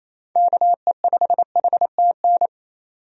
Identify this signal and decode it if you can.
Morse code, heard as a keyed tone, about 28 wpm, 700 Hz. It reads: KE5HTN
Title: USA Call Signs - 6 Characters - 28wpm